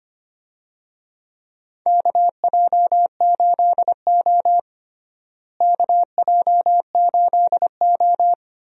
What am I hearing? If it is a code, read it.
Morse code, heard as a keyed tone, about 25 wpm, 700 Hz. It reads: KJ8O KJ8O
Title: USA Call Signs - 4 Characters - Repeated 2x - 25wpm